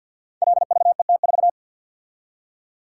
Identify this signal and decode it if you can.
Morse code, heard as a keyed tone, about 52 wpm, 700 Hz. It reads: LVET4
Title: Code Groups - 5 Letter:Numbers - 50wpm